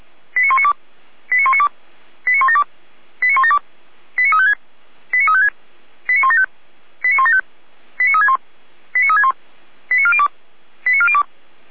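Five-tone selective call (ZVEI), heard as quick runs of five tones, tones of about 70 ms. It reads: 89192 89192 89172 89172 89356 89356 89175 89175 89361 89361 89402 89402
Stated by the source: ZVEI five-tone call, address 89391, received on an SDR.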